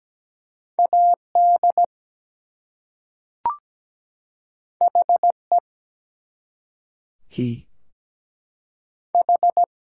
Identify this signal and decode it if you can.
Morse code, heard as a keyed tone, about 17 wpm, 700 Hz. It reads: AD HE H
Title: Two Letter Words - 17wpm